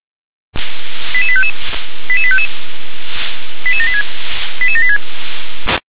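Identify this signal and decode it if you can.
Five-tone selective call (ZVEI), heard as quick runs of five tones, tones of about 70 ms. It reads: 88855 88855 88786 88786
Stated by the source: ZVEI five-tone selective call received on SDR.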